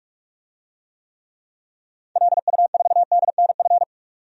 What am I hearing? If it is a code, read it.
Morse code, heard as a keyed tone, about 42 wpm, 700 Hz. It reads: LU4DNF
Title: DXCC-100 Call Signs - 6 Characters - 45wpm